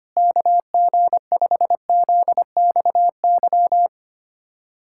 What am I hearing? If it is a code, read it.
Morse code, heard as a keyed tone, about 25 wpm, 700 Hz. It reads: KG5ZXY
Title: USA Callsign DE Callsign 25wpm